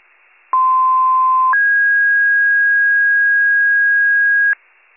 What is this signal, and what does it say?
Two-tone page, heard as a one-second tone, then a three-second tone, about 1040 then 1650 Hz. It is heard from 0.5 to 4.5 s.